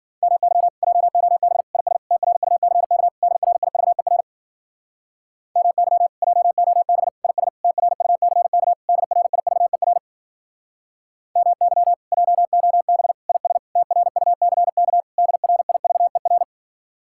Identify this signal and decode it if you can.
Morse code, heard as a keyed tone, about 49 wpm, 700 Hz. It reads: MY JOB IS TRUCK DRIVER MY JOB IS TRUCK DRIVER MY JOB IS TRUCK DRIVER
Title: Speed-Racing Common QSO Phrases Encore - 40wpm